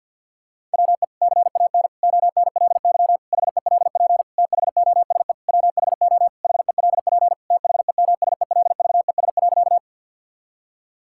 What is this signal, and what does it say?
Morse code, heard as a keyed tone, about 48 wpm, 700 Hz. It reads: WE CAN ONLY HELP THOSE WHO HELP THEMSELVES.
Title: Sentences from Top 1000 Words - Review Part 5 - 50wpm